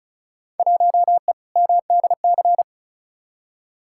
Morse code, heard as a keyed tone, about 36 wpm, 700 Hz. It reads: 1E MDC
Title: Field Day Exchanges - 35wpm